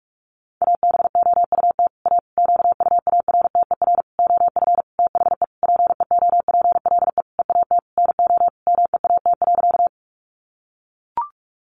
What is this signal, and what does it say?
Morse code, heard as a keyed tone, about 44 wpm, 700 Hz. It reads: ABOUT A QUARTER OF THE PEOPLE EAT NO MEAT.
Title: Sentences from Top 1300 Words 45wpm